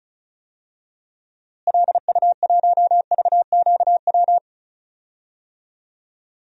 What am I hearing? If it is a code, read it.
Morse code, heard as a keyed tone, about 36 wpm, 700 Hz. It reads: LU1VQW
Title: DXCC-100 Call Signs - 6 Characters - Repeated 2x - 35wpm